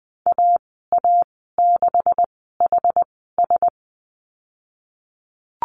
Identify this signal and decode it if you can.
Morse code, heard as a keyed tone, about 20 wpm, spaced as for 15 wpm, 700 Hz. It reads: AA6HS
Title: 128 Lesson 21g - Number 6 - Call Signs 20wpm